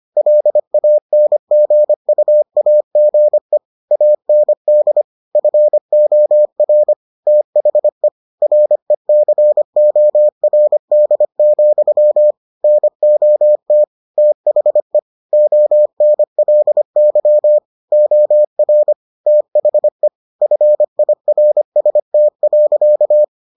Morse code, heard as a keyed tone, about 25 wpm, 600 Hz. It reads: LANGUAGE AND FOR THE RECORD, NOT THE ONLY OR THE FIRST.